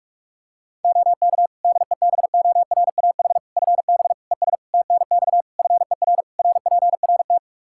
Morse code, heard as a keyed tone, about 41 wpm, 700 Hz. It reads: OK DEBORAH FB ES TNX FER RPRT
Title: Practice QSO - Exchange 2 - Info - 45wpm